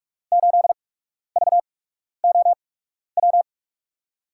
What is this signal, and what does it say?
Morse code, heard as a keyed tone, about 41 wpm, spaced as for 12 wpm, 700 Hz. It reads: 8VOW
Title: USA Call Signs - 6 Characters - 8x Character Spacing - 45wpm